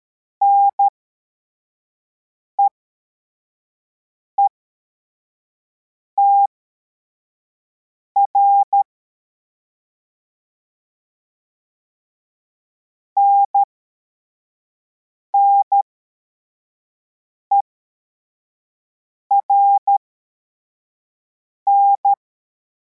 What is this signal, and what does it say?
Morse code, heard as a keyed tone, about 13 wpm, spaced as for 4 wpm, 800 Hz. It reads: NEETR NNERN